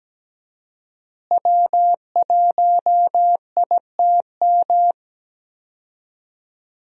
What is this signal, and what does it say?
Morse code, heard as a keyed tone, about 17 wpm, 700 Hz. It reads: W1ITM
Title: USA Call Signs - 5 Characters - Repeated 2x - 17wpm